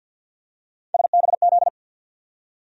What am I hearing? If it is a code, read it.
Morse code, heard as a keyed tone, about 52 wpm, 700 Hz. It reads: SBZ
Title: USA Call Sign Postfixes - 3 Letters 50wpm